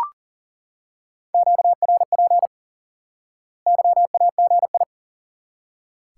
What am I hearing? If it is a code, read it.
Morse code, heard as a keyed tone, about 38 wpm, 700 Hz. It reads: QRP YAGI
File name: IQE Course 3 - Lesson 89 - YAGI - Two Elements - 40wpm